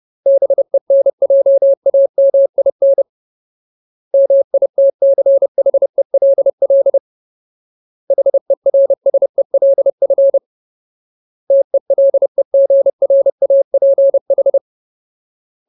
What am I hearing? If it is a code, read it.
Morse code, heard as a keyed tone, about 30 wpm, 550 Hz. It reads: BENJAMIN MITCHELL HERSELF TELEGRAPH
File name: Ultimate Word List / 30WPM Ultimate